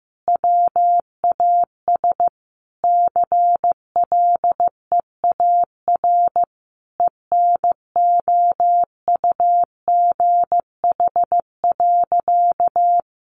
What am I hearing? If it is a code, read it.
Morse code, heard as a keyed tone, about 15 wpm, 700 Hz. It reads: WAS CLEAR ENOUGH.
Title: Sentences from Top 1500 Words 15wpm